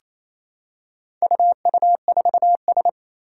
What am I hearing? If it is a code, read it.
Morse code, heard as a keyed tone, about 27 wpm, 700 Hz. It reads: UU4S
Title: Four Challenging Letter-Number Combinations 28wpm